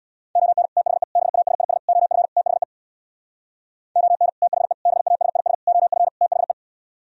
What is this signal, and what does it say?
Morse code, heard as a keyed tone, about 64 wpm, 700 Hz. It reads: ON THE BASIS OF THE ON THE BASIS OF THE